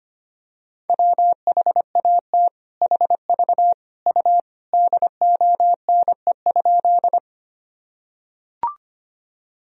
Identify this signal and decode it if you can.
Morse code, heard as a keyed tone, about 25 wpm, 700 Hz. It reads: WHAT HV U DONE?
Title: Sentences from Top 500 Words and Abbreviations 25wpm